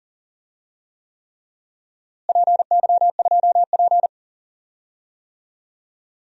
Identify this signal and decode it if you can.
Morse code, heard as a keyed tone, about 40 wpm, 700 Hz. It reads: PY2P
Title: DXCC-100 Call Signs - 4 Characters - Repeated 3x - 40wpm